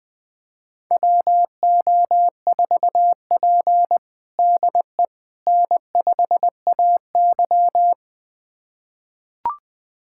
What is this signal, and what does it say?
Morse code, heard as a keyed tone, about 20 wpm, 700 Hz. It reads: WO4P DE N5AY
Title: USA Callsign DE Callsign - 4 Characters - 20wpm